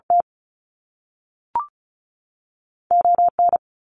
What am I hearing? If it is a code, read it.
Morse code, heard as a keyed tone, about 36 wpm, 700 Hz. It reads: T O N
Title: Canadian Provinces and Territories 35wpm